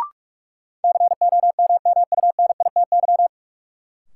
Morse code, heard as a keyed tone, about 46 wpm, 700 Hz. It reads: COMMUNITY
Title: Sentences from Top 1200 Words - New Word Drill Alt - 45wpm